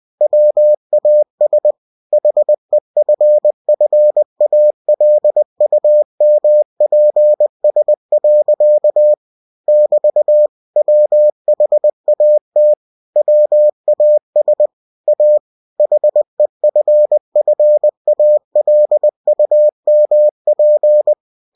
Morse code, heard as a keyed tone, about 20 wpm, 600 Hz. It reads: WAS HEFFALUMPS. <BT> WHAT WAS A HEFFALUMP
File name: cw-books / winnie-the-pooh / winnie-20wpm-page-0040